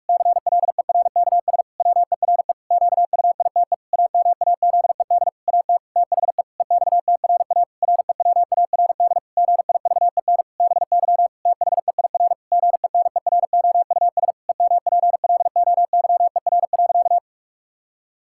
Morse code, heard as a keyed tone, about 46 wpm, 700 Hz. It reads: CLERKS WERE QUITE AMAZED AT THE EXTRA REWARD GIVEN BY THEIR GENEROUS EMPLOYER.
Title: Pangram Challenge v2 - 45wpm